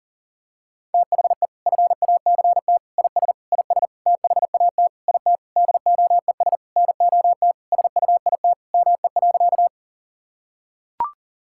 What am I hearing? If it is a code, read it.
Morse code, heard as a keyed tone, about 40 wpm, 700 Hz. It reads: THE FACT IS IS THAT IT DOES NOT SUIT ME.
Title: Sentences from Top 1600 Words 40wpm